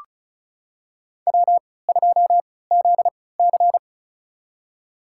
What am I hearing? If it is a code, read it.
Morse code, heard as a keyed tone, about 34 wpm, spaced as for 20 wpm, 700 Hz. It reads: W2ZC
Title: USA Call Signs - 3x Character Spacing - 35wpm